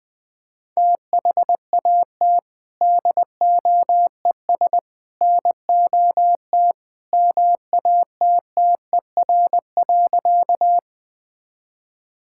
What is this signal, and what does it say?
Morse code, heard as a keyed tone, about 20 wpm, 700 Hz. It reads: THAT DOES NOT MATTER.
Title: Sentences with 3-Word Phrases - Spoken Beginning - 20wpm